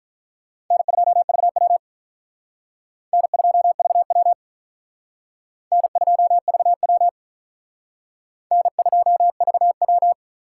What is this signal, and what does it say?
Morse code, heard as a keyed tone, about 46 wpm, 700 Hz. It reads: N2VW N2VW N2VW N2VW